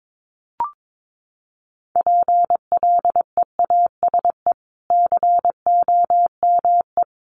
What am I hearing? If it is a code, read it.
Morse code, heard as a keyed tone, about 22 wpm, 700 Hz. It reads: PLEASE COME
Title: Sentences from Top 1500 Words 22wpm